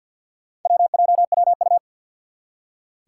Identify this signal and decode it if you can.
Morse code, heard as a keyed tone, about 47 wpm, 700 Hz. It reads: WJWU
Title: Four Challenging Letter-Number Combinations 50wpm